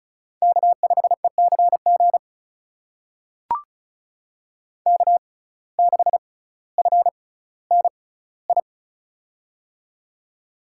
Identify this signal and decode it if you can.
Morse code, heard as a keyed tone, about 36 wpm, spaced as for 18 wpm, 700 Hz. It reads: K5ECG K6FNI
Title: USA Call Signs - 5 Characters - 6x Character Spacing - 35wpm